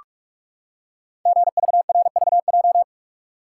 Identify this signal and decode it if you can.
Morse code, heard as a keyed tone, about 43 wpm, 700 Hz. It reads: GVRVJ
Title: Code Groups - 5 Letter:Numbers - 45wpm